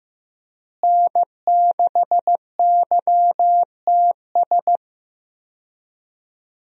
Morse code, heard as a keyed tone, about 15 wpm, 700 Hz. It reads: N6YTS
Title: USA Call Signs - 5 Characters - Repeated 3x - 15wpm